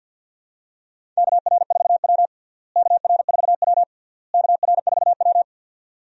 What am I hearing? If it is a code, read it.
Morse code, heard as a keyed tone, about 52 wpm, 700 Hz. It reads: KR4W KR4W KR4W
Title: USA Call Signs - 4 Characters - Repeated 3x - 50wpm